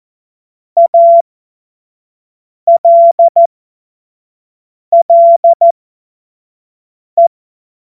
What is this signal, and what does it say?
Morse code, heard as a keyed tone, about 14 wpm, spaced as for 5 wpm, 680 Hz. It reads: ALLE
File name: Morsekurs / Witze / Witz 08. August 2023 25 F70